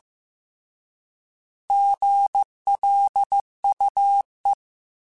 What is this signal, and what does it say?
Morse code, heard as a keyed tone, about 15 wpm, 800 Hz. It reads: GLUE
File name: Alphabetical / Alphabetical mp3s / Gs @13